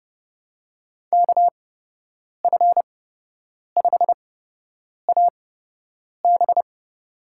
Morse code, heard as a keyed tone, about 30 wpm, spaced as for 8 wpm, 700 Hz. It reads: KF5AB
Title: USA Call Signs - 5 Characters - 8x Character Spacing - 30wpm